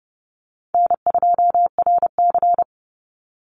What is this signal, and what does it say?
Morse code, heard as a keyed tone, about 30 wpm, 700 Hz. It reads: N2RC
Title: Easy Call Signs Repeated 3x - 30wpm